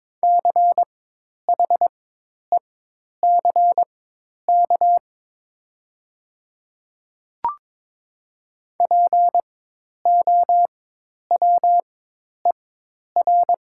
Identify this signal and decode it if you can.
Morse code, heard as a keyed tone, about 22 wpm, spaced as for 10 wpm, 700 Hz. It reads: CHECK POWER
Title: 4x Character Spacing - Top 500 Words - 22wpm